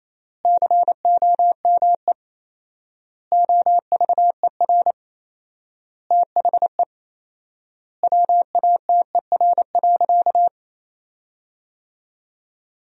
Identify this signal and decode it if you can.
Morse code, heard as a keyed tone, about 28 wpm, 700 Hz. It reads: COME OVER THE WATER.
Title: 4x Word-Spacing - Sentences from Top 300 Words - 28wpm